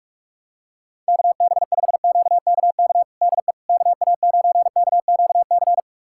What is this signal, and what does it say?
Morse code, heard as a keyed tone, about 46 wpm, 700 Hz. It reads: KB5QKK DE KA9KQC
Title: USA Callsign DE Callsign - 6 Characters - 45wpm